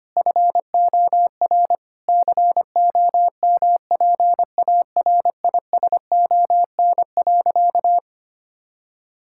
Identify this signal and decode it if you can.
Morse code, heard as a keyed tone, about 25 wpm, 700 Hz. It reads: FOR COMPARISON.
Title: Sentences from Top 1600 Words 25wpm